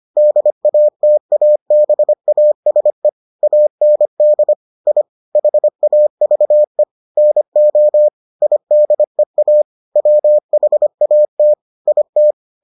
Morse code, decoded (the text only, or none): DATABASE AND I HAVE NO IDEA WHAT IT